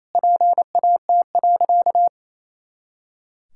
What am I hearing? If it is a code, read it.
Morse code, heard as a keyed tone, about 28 wpm, 700 Hz. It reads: PAT.
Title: Pangram Challenge v2 - 28wpm